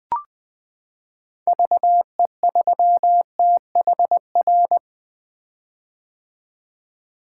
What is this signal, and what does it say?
Morse code, heard as a keyed tone, about 20 wpm, 700 Hz. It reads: VE3THR
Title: 2023 POTA Hunter Call Signs - 20wpm